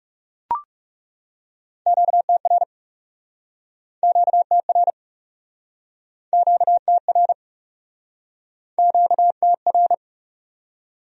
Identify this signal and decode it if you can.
Morse code, heard as a keyed tone, about 37 wpm, 700 Hz. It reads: QTR QTR QTR QTR
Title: Speed-Racing Q-Codes - 30wpm